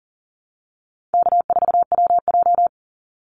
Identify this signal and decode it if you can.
Morse code, heard as a keyed tone, about 40 wpm, 700 Hz. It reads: K4WJ
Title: Easy Call Signs - 40wpm